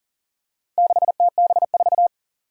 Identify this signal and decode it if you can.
Morse code, heard as a keyed tone, about 40 wpm, 700 Hz. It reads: 6TB4
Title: Four Challenging Letter-Number Combinations 40wpm